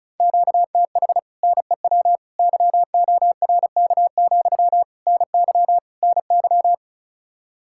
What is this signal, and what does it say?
Morse code, heard as a keyed tone, about 44 wpm, 700 Hz. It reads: QTH NEW YORK, NY NY